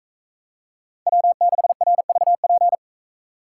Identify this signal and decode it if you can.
Morse code, heard as a keyed tone, about 44 wpm, 700 Hz. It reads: W6RVP